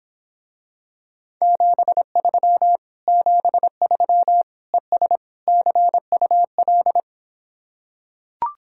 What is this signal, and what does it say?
Morse code, heard as a keyed tone, about 26 wpm, 700 Hz. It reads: 73 73 ES CUL